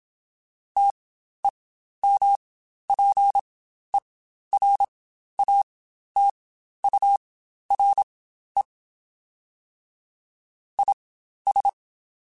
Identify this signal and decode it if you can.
Morse code, heard as a keyed tone, about 27 wpm, spaced as for 12 wpm, 800 Hz. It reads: TEMPERATURE IS
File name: Spaced Characters / Spaced Characters-mp3s / Weather 5 1 sp @25